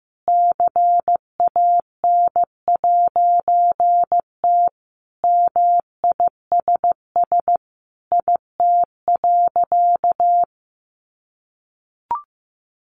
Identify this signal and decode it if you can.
Morse code, heard as a keyed tone, about 15 wpm, 700 Hz. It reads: CAN'T MISS IT.
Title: A Sound Of Thunder 15wpm